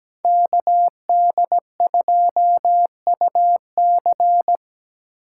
Morse code, heard as a keyed tone, about 20 wpm, 700 Hz. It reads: KD2UC
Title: Speed Racing - USA Call Signs 17wpm